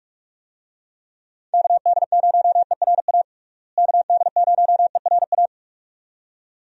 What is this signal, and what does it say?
Morse code, heard as a keyed tone, about 46 wpm, 700 Hz. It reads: KD0ERA KD0ERA
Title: USA Call Signs - 6 Characters - Repeated 2x - 45wpm